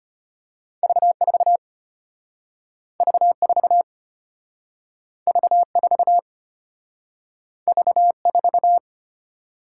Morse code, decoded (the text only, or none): V4 V4 V4 V4